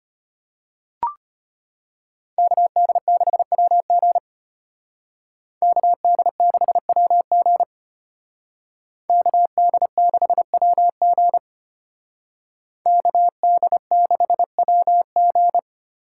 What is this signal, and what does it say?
Morse code, heard as a keyed tone, about 33 wpm, 700 Hz. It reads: KD6WG KD6WG KD6WG KD6WG